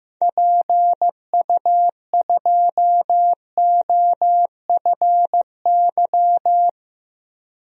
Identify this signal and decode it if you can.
Morse code, heard as a keyed tone, about 15 wpm, 690 Hz. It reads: PU2OFY
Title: DXCC-100 Call Signs - 6 Characters - 15wpm